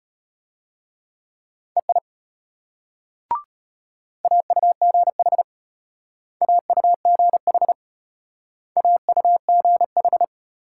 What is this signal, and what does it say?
Morse code, heard as a keyed tone, about 31 wpm, 700 Hz. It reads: EI AUGH AUGH AUGH
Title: Speed-Racing Multi-letter Phonemes - 25wpm